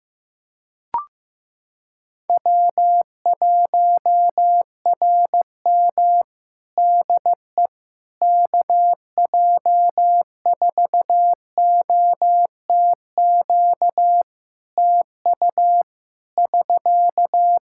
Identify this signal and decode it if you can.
Morse code, heard as a keyed tone, about 16 wpm, 700 Hz. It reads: W1RM DE KJ4OTQ TU <SK>
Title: Practice QSO - Exchange 4 - Ending - 15wpm